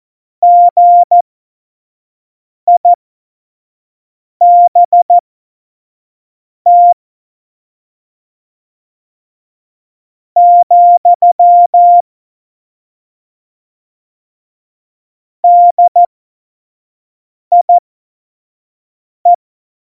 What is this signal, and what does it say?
Morse code, heard as a keyed tone, about 14 wpm, spaced as for 4 wpm, 700 Hz. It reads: GIBT, DIE
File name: Morsekurs / Witze / Witz 04. Dezember 2018, 25 F70